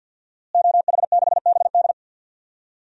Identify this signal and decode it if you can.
Morse code, heard as a keyed tone, about 52 wpm, 700 Hz. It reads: OH6BD
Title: DXCC-100 Call Signs - 5 Characters - Repeated 3x - 50wpm